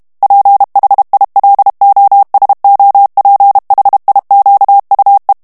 Morse code, heard as a keyed tone, about 32 wpm, 800 Hz. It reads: PHILOSOPHIQUE
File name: FRANCAIS TEXTE CLAIR 2 VITESSE 30 MOTS MINUTES